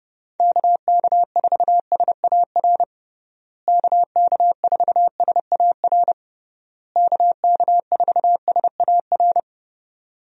Morse code, heard as a keyed tone, about 30 wpm, 700 Hz. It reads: KK4SAR KK4SAR KK4SAR
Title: USA Call Signs - 6 Characters - Repeated 3x - 30wpm